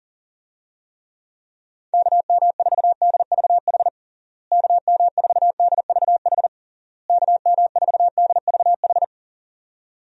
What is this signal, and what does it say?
Morse code, heard as a keyed tone, about 37 wpm, 700 Hz. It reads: KM4DVH KM4DVH KM4DVH
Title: USA Call Signs - 6 Characters - Repeated 3x - 40wpm